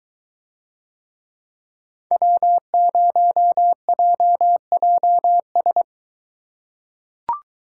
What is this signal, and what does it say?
Morse code, heard as a keyed tone, about 22 wpm, 700 Hz. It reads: W0JJS